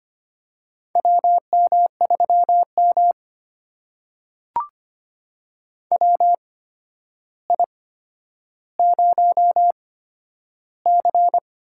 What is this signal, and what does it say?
Morse code, heard as a keyed tone, about 25 wpm, spaced as for 11 wpm, 700 Hz. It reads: WM3M WI0C
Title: USA Call Signs - 4 Characters - 8x Character Spacing - 25wpm